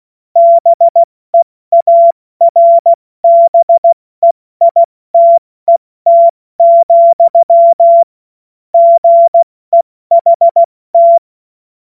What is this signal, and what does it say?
Morse code, heard as a keyed tone, about 16 wpm, spaced as for 14 wpm, 680 Hz. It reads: BEARBEITET, GEHT
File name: Witz-08-Juli-2025-70-F80